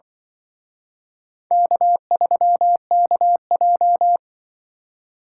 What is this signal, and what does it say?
Morse code, heard as a keyed tone, about 24 wpm, 700 Hz. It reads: K3KJ
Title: Speed Racing - USA 1x2 Call Signs 20wpm